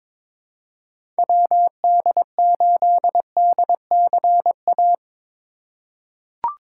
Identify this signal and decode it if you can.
Morse code, heard as a keyed tone, about 22 wpm, 700 Hz. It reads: WD8DCA
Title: USA Call Signs - 3x Character Spacing - 22wpm